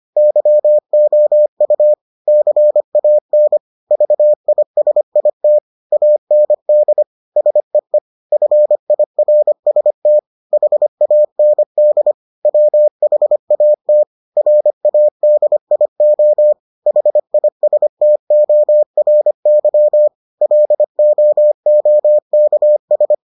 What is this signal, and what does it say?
Morse code, heard as a keyed tone, about 25 wpm, 600 Hz. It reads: YOU CAN VISIT AND SEE FIRST HAND WHAT RADIO HISTORY LOOKS